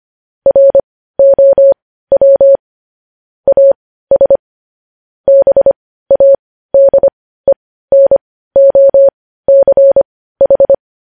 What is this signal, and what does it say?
Morse code, heard as a keyed tone, about 25 wpm, spaced as for 15 wpm, 550 Hz. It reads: ROW AS BADENOCH
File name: podcast / News headlines-1x-25x15